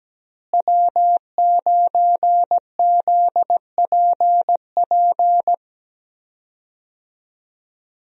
Morse code, heard as a keyed tone, about 17 wpm, 700 Hz. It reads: W9ZPP
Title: USA Call Signs - 5 Characters - Repeated 3x - 17wpm